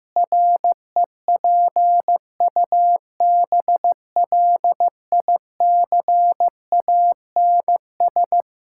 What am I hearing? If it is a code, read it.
Morse code, heard as a keyed tone, about 15 wpm, 700 Hz. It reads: REPUBLICANS
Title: Pangram Challenge v2 - 15wpm